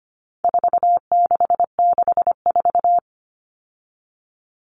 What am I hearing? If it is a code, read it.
Morse code, heard as a keyed tone, about 25 wpm, 700 Hz. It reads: 4664
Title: Numbers - 4-Digit 25wpm